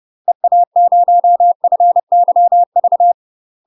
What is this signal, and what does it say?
Morse code, heard as a keyed tone, about 30 wpm, 700 Hz. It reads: EA0FYV
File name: UK MASTER QSOs TRAINING / 30WPM / UK MASTER QSOs 0029